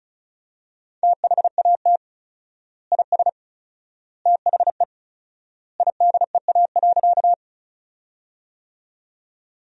Morse code, heard as a keyed tone, about 36 wpm, 700 Hz. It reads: THAT IS THE IDEA.
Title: 4x Word-Spacing - Sentences from Top 300 Words - 35wpm